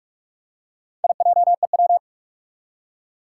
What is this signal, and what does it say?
Morse code, heard as a keyed tone, about 44 wpm, 700 Hz. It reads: IJEW
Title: Four Challenging Letter-Number Combinations 45wpm